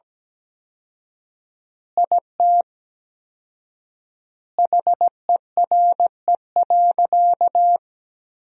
Morse code, heard as a keyed tone, about 17 wpm, 700 Hz. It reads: IT HERE.
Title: 4x Word-Spacing - Sentences from Top 400 Words - 17wpm